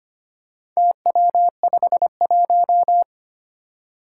Morse code, heard as a keyed tone, about 25 wpm, 700 Hz. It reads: TW51
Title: Four Challenging Letter-Number Combinations 25wpm